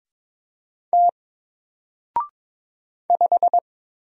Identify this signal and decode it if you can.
Morse code, heard as a keyed tone, about 22 wpm, 700 Hz. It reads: T 5
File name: Single-Letter-Number-22wpm